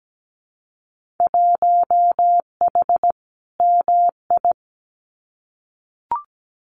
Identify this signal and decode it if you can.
Morse code, heard as a keyed tone, about 17 wpm, 700 Hz. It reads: 1H MI
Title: Winter Field Day Exchanges 17wpm